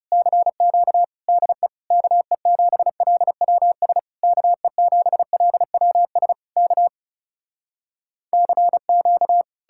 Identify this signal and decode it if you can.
Morse code, heard as a keyed tone, about 40 wpm, 700 Hz. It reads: CQ DE KE7LWS KE7LWS K CQ